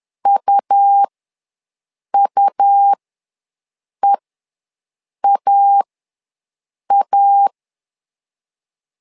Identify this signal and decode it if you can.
Morse code, heard as a keyed tone, about 11 wpm, spaced as for 6 wpm, 800 Hz. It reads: UUEAA
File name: LEZIONI / 2 UHDN / 2 U ws 10wpm 2017